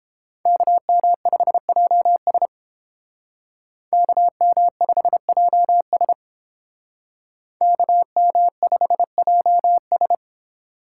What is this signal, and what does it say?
Morse code, heard as a keyed tone, about 30 wpm, 700 Hz. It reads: KM5JS KM5JS KM5JS